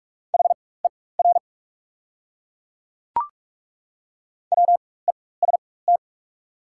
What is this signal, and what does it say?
Morse code, heard as a keyed tone, about 42 wpm, spaced as for 20 wpm, 690 Hz. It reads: HER WEST
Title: 4x Character Spacing - Top 500 Words - 45wpm